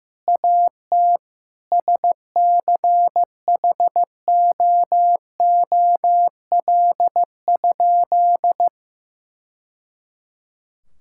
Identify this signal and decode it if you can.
Morse code, heard as a keyed tone, about 15 wpm, 700 Hz. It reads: AT SCHOOL?
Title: Sentences from Top 1000 Words - Review Part 6 - 15wpm